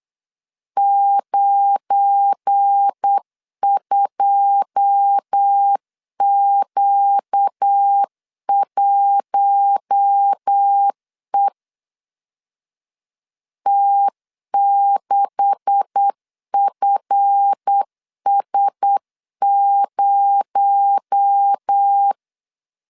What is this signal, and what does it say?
Morse code, heard as a keyed tone, about 8 wpm, 800 Hz. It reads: 92Q1E T6FS0
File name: LEZIONI / 9 286 8wpm 3425